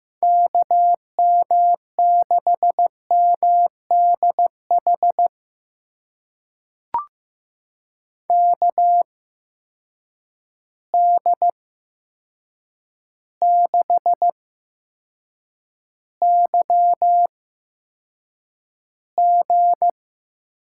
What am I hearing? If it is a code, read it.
Morse code, heard as a keyed tone, about 15 wpm, spaced as for 7 wpm, 700 Hz. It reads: KM6MDH KD6YG
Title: USA Call Signs - 6 Characters - 8x Character Spacing - 15wpm